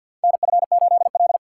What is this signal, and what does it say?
Morse code, heard as a keyed tone, about 51 wpm, 700 Hz. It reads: NF8L
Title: USA Call Signs - 4 Characters - 3x Character Spacing - 50wpm